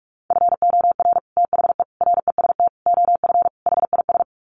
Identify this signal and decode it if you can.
Morse code, heard as a keyed tone, about 44 wpm, 700 Hz. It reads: FOR THE REST OF HIS
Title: Sets of 5 Words 45wpm